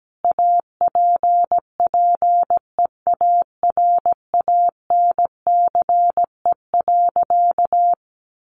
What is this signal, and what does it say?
Morse code, heard as a keyed tone, about 17 wpm, 700 Hz. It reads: APPEARANCE.
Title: Sentences from Top 1100 Words 17wpm